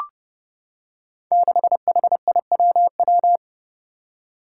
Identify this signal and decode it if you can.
Morse code, heard as a keyed tone, about 29 wpm, 700 Hz. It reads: 6HIWW
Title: Five Challenging Letter-Number Combinations 30wpm